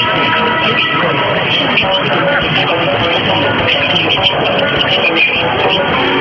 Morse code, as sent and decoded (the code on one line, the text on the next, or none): none